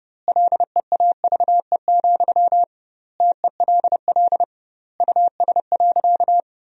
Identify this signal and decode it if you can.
Morse code, heard as a keyed tone, about 32 wpm, 700 Hz. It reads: LEAVE, TELL US.
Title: 2x Word-Spacing - Sentences from Top 400 Words - 30wpm